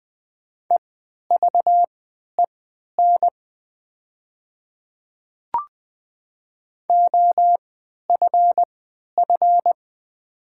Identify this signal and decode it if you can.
Morse code, heard as a keyed tone, about 20 wpm, spaced as for 11 wpm, 700 Hz. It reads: EVEN OFF
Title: 3x Character Spacing - Top 200 Words - 20wpm